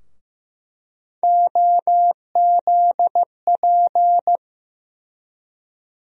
Morse code, heard as a keyed tone, about 15 wpm, 700 Hz. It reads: OZP
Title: USA Call Sign Postfixes - 3 Letters 15wpm